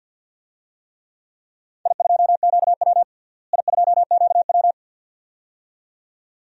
Morse code, heard as a keyed tone, about 52 wpm, 700 Hz. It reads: I2QW I2QW
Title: DXCC-100 Call Signs - 4 Characters - Repeated 2x - 50wpm